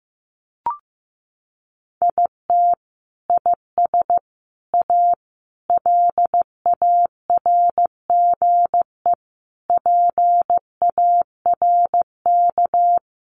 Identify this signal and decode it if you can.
Morse code, heard as a keyed tone, about 14 wpm, 700 Hz. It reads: IT IS A LARGE PARK
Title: Sentences from Top 1300 Words 15wpm